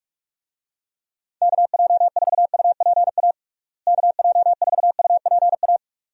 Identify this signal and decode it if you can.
Morse code, heard as a keyed tone, about 41 wpm, 700 Hz. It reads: KJ4UPA KJ4UPA
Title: USA Call Signs - 6 Characters - Repeated 2x - 45wpm